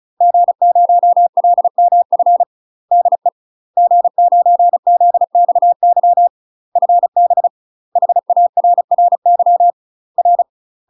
Morse code, heard as a keyed tone, about 34 wpm, 700 Hz. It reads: G0LMF DE G9ZXY FB HARRY R
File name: UK MASTER QSOs TRAINING / 35WPM / UK MASTER QSOs 0012